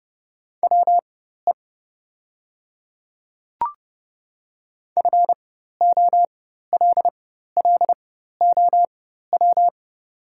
Morse code, heard as a keyed tone, about 30 wpm, spaced as for 14 wpm, 700 Hz. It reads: WE FOLLOW
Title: 4x Character Spacing - Top 200 Words - 30wpm